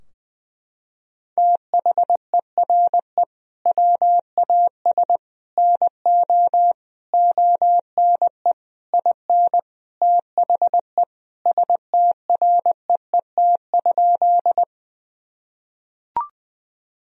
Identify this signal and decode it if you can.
Morse code, heard as a keyed tone, about 20 wpm, 700 Hz. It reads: THERE WAS NO ONE IN THE STREET?
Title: Sentences from Top 1000 Words - Review Part 3 - 20wpm